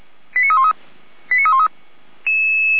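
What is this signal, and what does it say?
Five-tone selective call (ZVEI), heard as quick runs of five tones, tones of about 70 ms. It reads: 89313 89313
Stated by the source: ZVEI five-tone selective call received on SDR.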